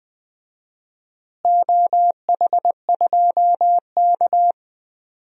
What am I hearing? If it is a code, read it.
Morse code, heard as a keyed tone, about 20 wpm, 700 Hz. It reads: OH2K
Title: DXCC-100 Call Signs - 4 Characters - Repeated 2x - 20wpm